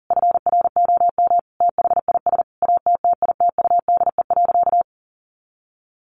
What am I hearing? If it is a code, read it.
Morse code, heard as a keyed tone, about 40 wpm, 700 Hz. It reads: FROM THIS ATTITUDE.
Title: Sentences from Top 1300 Words 40wpm